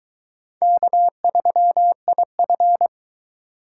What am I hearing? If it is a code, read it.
Morse code, heard as a keyed tone, about 24 wpm, 700 Hz. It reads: K3IF